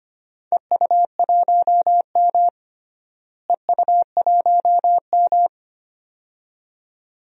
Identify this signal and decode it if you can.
Morse code, heard as a keyed tone, about 25 wpm, 700 Hz. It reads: EU1M EU1M
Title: DXCC-100 Call Signs - 4 Characters - Repeated 2x - 25wpm